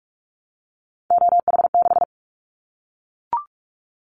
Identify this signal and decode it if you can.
Morse code, heard as a keyed tone, about 44 wpm, 700 Hz. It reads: OH6
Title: Callsign Prefixes for Top 25 DXCC - Repeated 3x - 45wpm